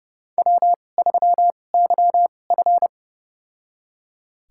Morse code, heard as a keyed tone, about 32 wpm, spaced as for 22 wpm, 700 Hz. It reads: W3YF
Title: USA Call Signs - 4 Characters - 2x Character Spacing - 30wpm